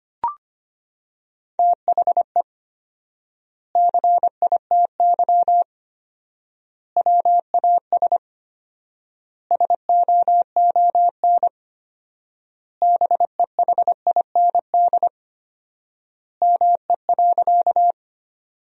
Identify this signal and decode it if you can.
Morse code, heard as a keyed tone, about 25 wpm, 700 Hz. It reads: THE CITY WAS SOON BEHIND ME.
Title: 4x Word-Spacing - Sentences from Top 500 Words - 25wpm